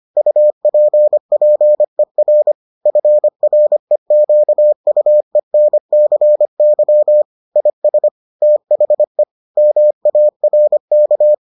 Morse code, heard as a keyed tone, about 25 wpm, 600 Hz. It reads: UPPER FREQUENCY IS THE MARK